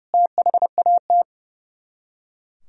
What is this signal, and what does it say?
Morse code, heard as a keyed tone, about 40 wpm, spaced as for 35 wpm, 700 Hz. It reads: THAT
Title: Speed-Racing - Sets of 5 Words 30wpm